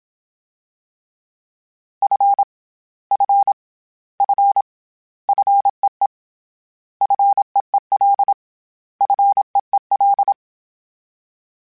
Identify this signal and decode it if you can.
Morse code, heard as a keyed tone, about 27 wpm, 800 Hz. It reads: F F F FEE FEEL FEEL
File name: Short Runs / Short Runs-mp3s / SinglePrefixes @25